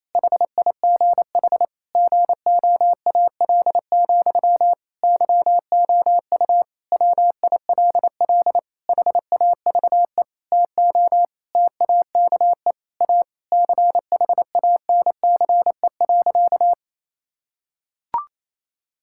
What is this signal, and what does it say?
Morse code, heard as a keyed tone, about 28 wpm, 700 Hz. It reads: HIGH GOAL, YOU WILL HAVE TO TAKE A CHANCE.
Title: Sentences from Top 1000 Words - Review Part 6 - 28wpm